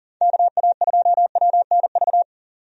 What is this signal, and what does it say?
Morse code, heard as a keyed tone, about 40 wpm, 700 Hz. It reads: KA2WNV
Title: USA Call Signs - 6 Characters - 40wpm